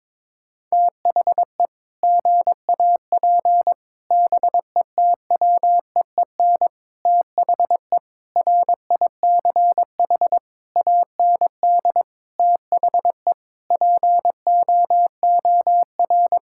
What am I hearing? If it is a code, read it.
Morse code, heard as a keyed tone, about 22 wpm, 700 Hz. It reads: THE GAP BETWEEN THE RICH AND THE POOR
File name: practice / 22wpm / Sentences with 8-Word Phrases - 22wpm